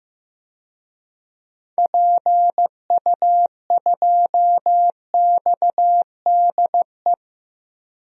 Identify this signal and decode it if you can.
Morse code, heard as a keyed tone, about 15 wpm, 700 Hz. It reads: PU2XDE
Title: DXCC-100 Call Signs - 6 Characters - 15wpm